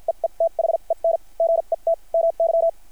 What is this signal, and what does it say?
Morse code, heard as a keyed tone, about 50 wpm, spaced as for 40 wpm, 660 Hz. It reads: EETHEN GET MY
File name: archived / CRSnet